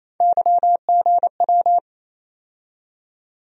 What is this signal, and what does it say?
Morse code, heard as a keyed tone, about 28 wpm, 700 Hz. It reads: YGW
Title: USA Call Sign Postfixes - 3 Letters 28wpm